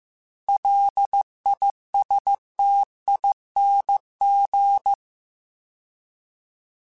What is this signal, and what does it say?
Morse code, heard as a keyed tone, about 15 wpm, 800 Hz. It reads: LISTING